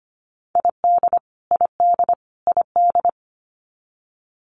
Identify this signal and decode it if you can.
Morse code, heard as a keyed tone, about 25 wpm, 700 Hz. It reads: ID ID ID
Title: US States Repeated 3x 25wpm